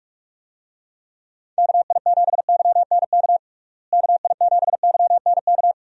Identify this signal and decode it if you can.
Morse code, heard as a keyed tone, about 46 wpm, 700 Hz. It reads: KI7YNK KI7YNK
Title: USA Call Signs - 6 Characters - Repeated 3x - 45wpm